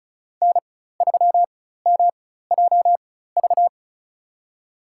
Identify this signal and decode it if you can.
Morse code, heard as a keyed tone, about 34 wpm, spaced as for 16 wpm, 700 Hz. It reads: N3MJV
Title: USA Call Signs - 5 Characters - 4x Character Spacing - 35wpm